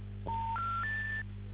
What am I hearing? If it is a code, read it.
Special information tone, with noise.